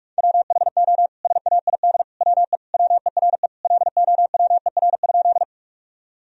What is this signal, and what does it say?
Morse code, heard as a keyed tone, about 46 wpm, 700 Hz. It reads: WHO SAID WE WERE LOWER?
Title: Sentences from Top 1000 Words - Review Part 4 - 45wpm